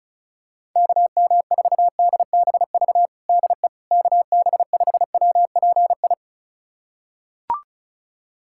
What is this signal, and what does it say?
Morse code, heard as a keyed tone, about 35 wpm, 700 Hz. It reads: KM4DBV DE KB5WPI
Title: USA Callsign DE Callsign - 6 Characters - 35wpm